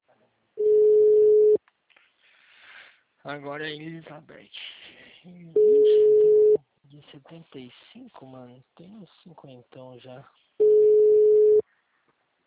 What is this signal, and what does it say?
Ringback tone (Europe, 425 Hz), with speech.
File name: out- -2018